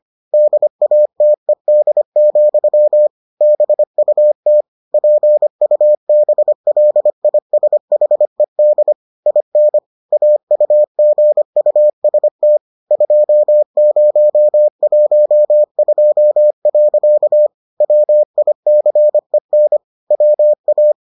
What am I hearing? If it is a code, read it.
Morse code, heard as a keyed tone, about 25 wpm, 600 Hz. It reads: DATED, BUT PUBLISHED IN AUGUST 2012. WICEN WA